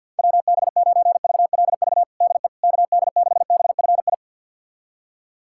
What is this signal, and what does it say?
Morse code, heard as a keyed tone, about 53 wpm, 700 Hz. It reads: WB9VLV DE KD6BFI
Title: USA Callsign DE Callsign 50wpm